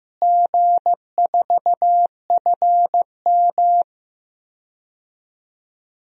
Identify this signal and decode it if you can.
Morse code, heard as a keyed tone, about 15 wpm, 700 Hz. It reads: G4FM
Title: DXCC-100 Call Signs - 4 Characters - Repeated 2x - 15wpm